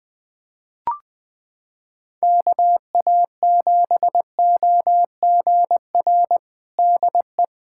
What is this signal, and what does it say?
Morse code, heard as a keyed tone, about 20 wpm, 700 Hz. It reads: KA7OGR DE
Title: USA Callsign DE Callsign - 6 Characters - 20wpm